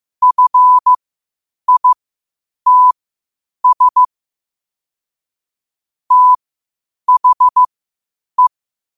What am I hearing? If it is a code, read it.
Morse code, heard as a keyed tone, about 15 wpm, spaced as for 8 wpm, 1000 Hz. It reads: FITS THE
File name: qotdfiles / 8wpm